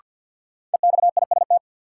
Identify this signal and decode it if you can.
Morse code, heard as a keyed tone, about 47 wpm, 700 Hz. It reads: EXIST
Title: Sentences from Top 1000 Words - Review Part 5 - New Word Drill - 50wpm